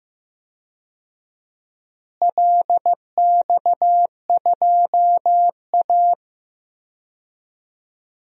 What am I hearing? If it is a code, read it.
Morse code, heard as a keyed tone, about 15 wpm, 700 Hz. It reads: LX2A
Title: DXCC-100 Call Signs - 4 Characters - Repeated 2x - 15wpm